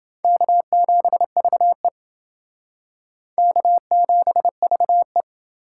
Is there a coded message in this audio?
Yes: Morse code from 0.2 to 5.2 s, heard as a keyed tone, about 28 wpm, 700 Hz. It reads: K7VE K7VE